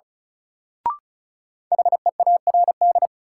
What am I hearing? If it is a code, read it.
Morse code, heard as a keyed tone, about 36 wpm, 700 Hz. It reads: HEARD
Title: Sentences from Top 400 Words - New Word Drill - 35wpm